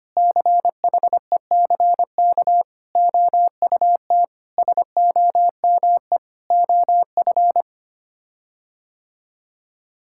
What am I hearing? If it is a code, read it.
Morse code, heard as a keyed tone, about 25 wpm, 700 Hz. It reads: CHECK OUT SOME OF
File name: Sets of 9 Words - 25wpm